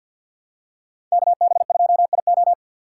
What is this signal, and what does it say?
Morse code, heard as a keyed tone, about 52 wpm, 700 Hz. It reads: KB2IO
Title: USA Call Signs - 5 Characters - 4x Character Spacing - 50wpm